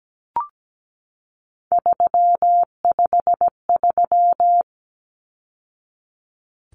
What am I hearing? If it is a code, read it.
Morse code, heard as a keyed tone, about 17 wpm, 700 Hz. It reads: 353
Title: Numbers - 3-Digit 17wpm